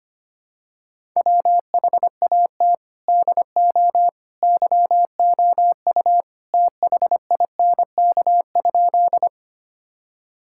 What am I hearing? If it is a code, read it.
Morse code, heard as a keyed tone, about 25 wpm, 700 Hz. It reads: WHAT DO YOU THINK?
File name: practice / 25wpm / Sentences with 2-Word Phrases - 25wpm